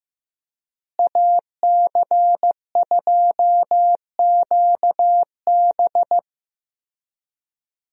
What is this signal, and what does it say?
Morse code, heard as a keyed tone, about 15 wpm, 700 Hz. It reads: AC2QB
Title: USA Call Signs - 5 Characters - Repeated 2x - 15wpm